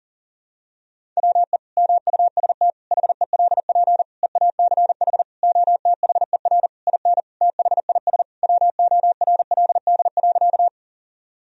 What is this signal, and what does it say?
Morse code, heard as a keyed tone, about 40 wpm, 700 Hz. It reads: WE MUST HELP EACH OTHER IN THIS WORLD.
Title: Sentences from Top 1000 Words - Review Part 1 - 40wpm